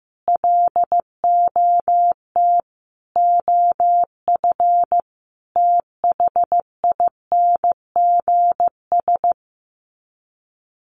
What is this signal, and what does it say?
Morse code, heard as a keyed tone, about 14 wpm, 700 Hz. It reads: LOT OF THINGS
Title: Sets of 4 Words 15wpm